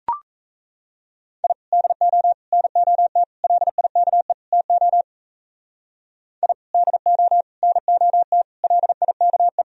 Morse code, heard as a keyed tone, about 35 wpm, 700 Hz. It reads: I DO NOT LIKE TO I DO NOT LIKE
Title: Speed-Racing - Sets of 5 Words 28wpm